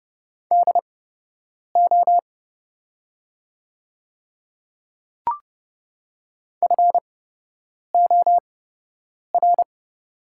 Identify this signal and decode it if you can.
Morse code, heard as a keyed tone, about 30 wpm, spaced as for 8 wpm, 700 Hz. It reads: DO FOR
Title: 8x Character Spacing - Top 100 Words - 30wpm